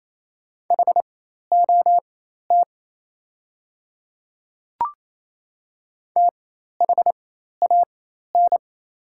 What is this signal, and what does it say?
Morse code, heard as a keyed tone, about 27 wpm, spaced as for 13 wpm, 700 Hz. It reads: HOT THAN
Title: 4x Character Spacing - Top 200 Words - 28wpm